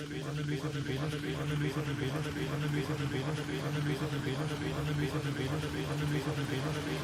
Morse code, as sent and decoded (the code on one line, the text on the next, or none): none